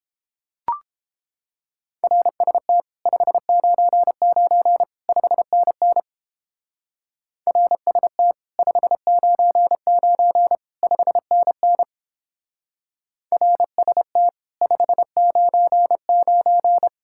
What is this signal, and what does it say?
Morse code, heard as a keyed tone, about 29 wpm, 700 Hz. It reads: RST 599 5NN RST 599 5NN RST 599